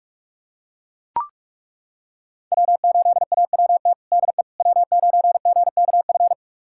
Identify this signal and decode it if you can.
Morse code, heard as a keyed tone, about 42 wpm, 700 Hz. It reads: W8AWT DE W9GKF
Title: USA Callsign DE Callsign - 5 Characters - 45wpm